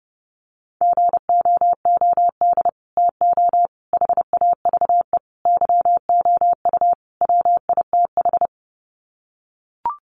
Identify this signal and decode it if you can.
Morse code, heard as a keyed tone, about 30 wpm, 700 Hz. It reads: GOOD TO HAVE YOU WITH
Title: Sets of 5 Words Encore2 30wpm